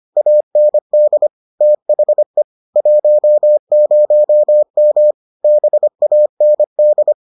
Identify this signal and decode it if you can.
Morse code, heard as a keyed tone, about 25 wpm, 600 Hz. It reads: AND THE 10M BAND